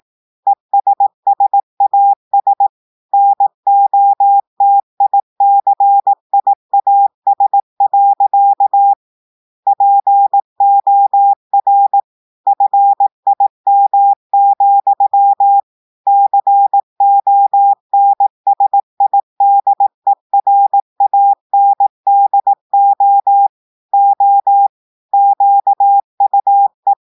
Morse code, heard as a keyed tone, about 18 wpm, 800 Hz. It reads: ESSAS NOTICIAS. POR FIM, CONSIDERANDO O QUE
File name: portugais texte en clair 2 vitesse 18 mots minutes